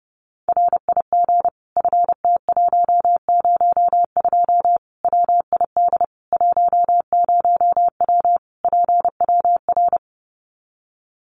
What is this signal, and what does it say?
Morse code, heard as a keyed tone, about 30 wpm, 700 Hz. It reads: RIG FT102 WID 10W PWR
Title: Common QSO Phrases 30wpm